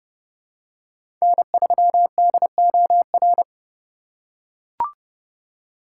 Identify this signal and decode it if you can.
Morse code, heard as a keyed tone, about 29 wpm, 700 Hz. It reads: N3DOR
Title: USA Call Signs - 5 Characters - 8x Character Spacing - 30wpm